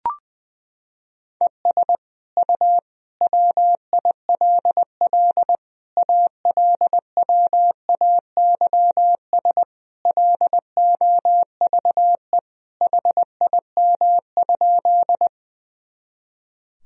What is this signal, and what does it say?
Morse code, heard as a keyed tone, about 20 wpm, 700 Hz. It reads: ES U WILL ALWAYS LOVE HIM?
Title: Sentences from Top 400 Words and Abbreviations 20wpm